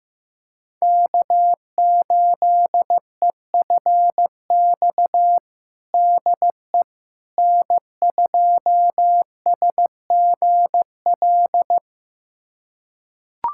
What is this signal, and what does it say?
Morse code, heard as a keyed tone, about 15 wpm, 700 Hz. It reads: K8EFX DE N2SGL
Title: USA Callsign DE Callsign - 5 Characters - 15wpm